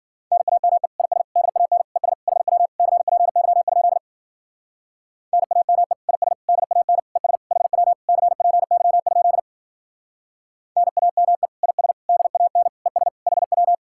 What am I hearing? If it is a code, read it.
Morse code, heard as a keyed tone, about 48 wpm, 700 Hz. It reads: NAME IS DAN ES HW CPY? NAME IS DAN ES HW CPY? NAME IS DAN ES HW
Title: Speed-Racing Common QSO Phrases Encore - 40wpm